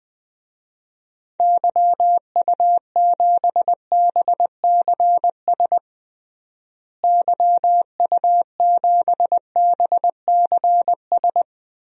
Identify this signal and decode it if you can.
Morse code, heard as a keyed tone, about 20 wpm, 700 Hz. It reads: YU7BCS YU7BCS
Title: DXCC-100 Call Signs - 6 Characters - Repeated 2x - 20wpm